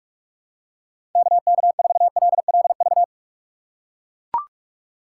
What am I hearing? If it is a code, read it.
Morse code, heard as a keyed tone, about 44 wpm, 700 Hz. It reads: KK4LLV
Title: USA Call Signs - 6 Characters - 6x Character Spacing - 45wpm